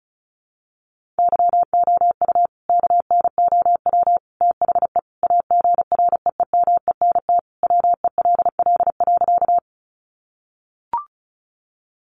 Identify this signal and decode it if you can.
Morse code, heard as a keyed tone, about 34 wpm, 700 Hz. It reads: YOU KNOW THE AGREEMENT WELL.
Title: Sentences from Top 1500 Words 35wpm